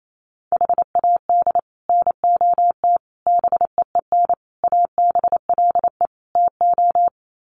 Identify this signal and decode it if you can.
Morse code, heard as a keyed tone, about 28 wpm, 700 Hz. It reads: HAD NOT BEEN ABLE TO
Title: Sets of 5 Words Encore 28wpm